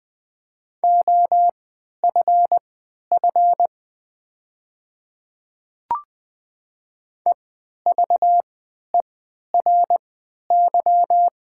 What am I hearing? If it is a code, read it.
Morse code, heard as a keyed tone, about 20 wpm, spaced as for 11 wpm, 690 Hz. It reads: OFF EVERY
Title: 3x Character Spacing - Top 200 Words - 20wpm